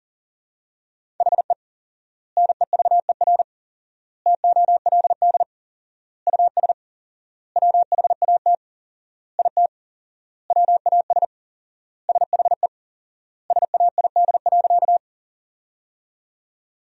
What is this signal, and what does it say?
Morse code, heard as a keyed tone, about 37 wpm, 700 Hz. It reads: HE NEVER TOLD US WHAT IT WAS SHE SAID.
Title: 4x Word-Spacing - Sentences from Top 400 Words - 40wpm